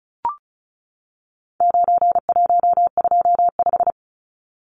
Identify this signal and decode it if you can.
Morse code, heard as a keyed tone, about 34 wpm, 700 Hz. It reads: 9125
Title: Numbers - 4-Digit 35wpm